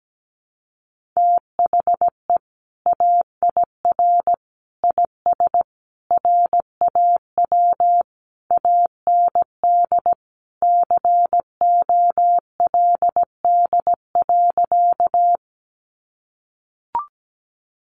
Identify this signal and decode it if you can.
Morse code, heard as a keyed tone, about 17 wpm, 700 Hz. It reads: THE AIR IS RAW AND COLD.
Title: Sentences from Top 1500 Words 17wpm